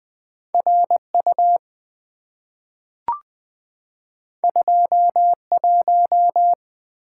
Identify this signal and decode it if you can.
Morse code, heard as a keyed tone, about 20 wpm, 700 Hz. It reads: RU 21
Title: Two Letter-Number Combinations - 20wpm